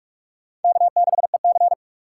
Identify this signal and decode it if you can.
Morse code, heard as a keyed tone, about 40 wpm, 700 Hz. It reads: K6EC